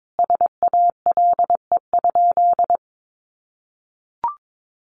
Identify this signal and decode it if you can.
Morse code, heard as a keyed tone, about 22 wpm, 700 Hz. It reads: SALE?
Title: Sentences from Top 1400 Words 22wpm